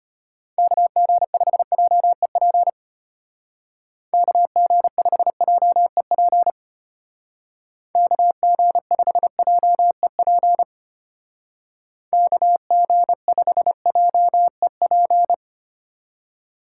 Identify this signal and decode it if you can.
Morse code, heard as a keyed tone, about 30 wpm, 700 Hz. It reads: KG5JEP KG5JEP KG5JEP KG5JEP
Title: Speed Racing - USA Call Signs 25wpm